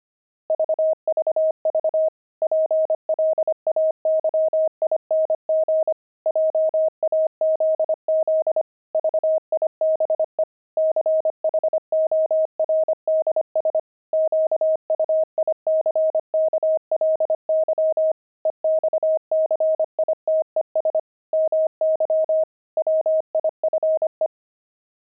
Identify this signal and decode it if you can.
Morse code, heard as a keyed tone, about 25 wpm, 630 Hz. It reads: VVV PLAYING JAZZ VIBE CHORDS QUICKLY EXCITES MY WIFE
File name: pangram-fs-25-cs-25-jazz